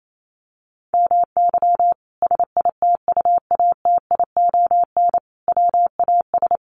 Morse code, heard as a keyed tone, about 28 wpm, 700 Hz. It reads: MY SITUATION WAS
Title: Sentences from Top 1200 Words 28wpm